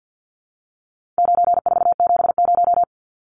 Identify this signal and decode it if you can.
Morse code, heard as a keyed tone, about 50 wpm, 700 Hz. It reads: 9470
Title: Numbers - 4-Digit 50wpm